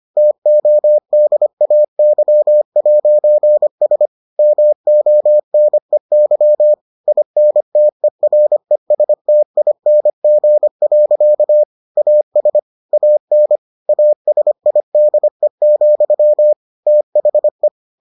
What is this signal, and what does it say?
Morse code, heard as a keyed tone, about 25 wpm, 600 Hz. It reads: TODAY'S MONEY INTERESTING. AS AN ASIDE, THE